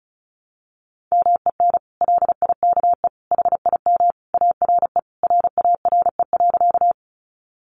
Morse code, heard as a keyed tone, about 34 wpm, 700 Hz. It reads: MEN LIKE HIM ARE RARE.
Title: Sentences from Top 1100 Words 35wpm